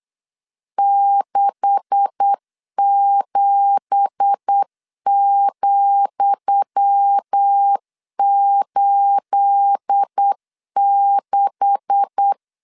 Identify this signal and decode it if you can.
Morse code, heard as a keyed tone, about 8 wpm, 800 Hz. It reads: 67,86
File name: LEZIONI / 10 SIGNS / 10D COMA 8wpm 3749